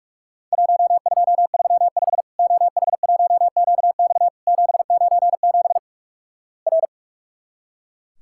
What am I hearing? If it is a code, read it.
Morse code, heard as a keyed tone, about 44 wpm, 700 Hz. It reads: 1235 OH1QX 797 R
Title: Worked All Europe DX Contest - QTC Exchange - 45wpm